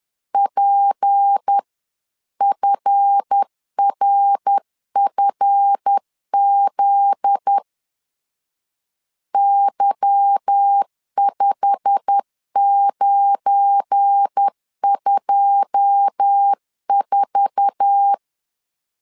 Morse code, heard as a keyed tone, about 10 wpm, 800 Hz. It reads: PFRFZ Y5924